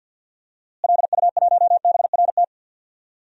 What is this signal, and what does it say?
Morse code, heard as a keyed tone, about 50 wpm, 700 Hz. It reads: LU1BRT
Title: DXCC-100 Call Signs - 6 Characters - Repeated 3x - 50wpm